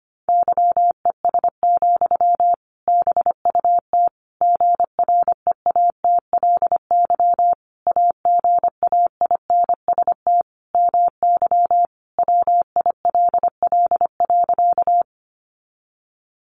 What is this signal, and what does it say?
Morse code, heard as a keyed tone, about 25 wpm, 700 Hz. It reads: YES, BUT GREATLY AGAINST MY WILL.
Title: Sentences from Top 1400 Words 25wpm